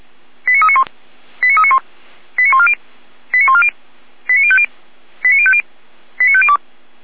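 Five-tone selective call (ZVEI), heard as quick runs of five tones, tones of about 70 ms. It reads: 89391 89391 89140 89140 89050 89050 89502
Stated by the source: ZVEI five-tone call 89092 received by SDR, from a Feuerwehr Orlamünde archive.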